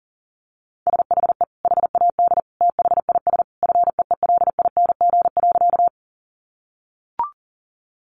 Morse code, heard as a keyed tone, about 40 wpm, 700 Hz. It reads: SHE HAD THIS FEELING.
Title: Sentences from Top 1200 Words 40wpm